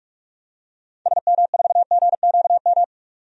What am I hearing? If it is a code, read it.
Morse code, heard as a keyed tone, about 42 wpm, 700 Hz. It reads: SM4GQM
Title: DXCC-100 Call Signs - 6 Characters - 45wpm